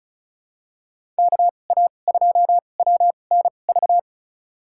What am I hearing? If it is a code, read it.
Morse code, heard as a keyed tone, about 34 wpm, spaced as for 25 wpm, 700 Hz. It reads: KA2WNV
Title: USA Call Signs - 2x Character Spacing - 35wpm